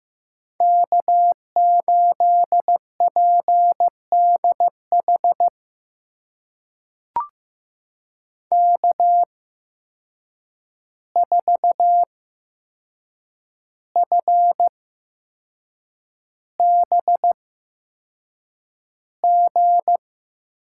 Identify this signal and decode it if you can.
Morse code, heard as a keyed tone, about 15 wpm, spaced as for 6 wpm, 700 Hz. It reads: K8PDH K4FBG
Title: USA Call Signs - 5 Characters - 8x Character Spacing - 15wpm